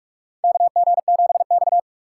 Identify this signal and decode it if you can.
Morse code, heard as a keyed tone, about 46 wpm, 700 Hz. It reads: KG7X
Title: USA Call Signs - 4 Characters - 4x Character Spacing - 45wpm